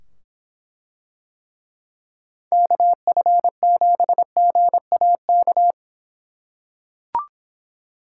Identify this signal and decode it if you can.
Morse code, heard as a keyed tone, about 24 wpm, 700 Hz. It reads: KF7GAK